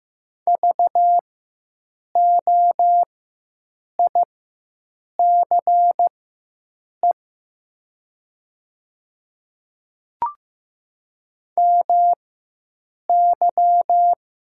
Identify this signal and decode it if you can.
Morse code, heard as a keyed tone, about 15 wpm, spaced as for 7 wpm, 700 Hz. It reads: VOICE MY
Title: 4x Character Spacing - Top 500 Words - 15wpm